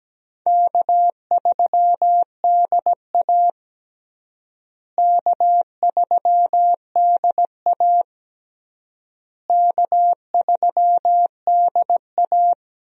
Morse code, heard as a keyed tone, about 17 wpm, 700 Hz. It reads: K3DA K3DA K3DA
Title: USA Call Signs - 4 Characters - Repeated 3x - 17wpm